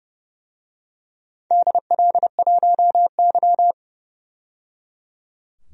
Morse code, heard as a keyed tone, about 30 wpm, 700 Hz. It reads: DL1Y
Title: DXCC-100 Call Signs - 4 Characters - 30wpm